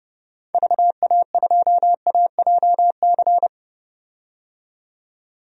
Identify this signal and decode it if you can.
Morse code, heard as a keyed tone, about 30 wpm, 700 Hz. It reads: VA2AJC
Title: DXCC-100 Call Signs - 6 Characters - 30wpm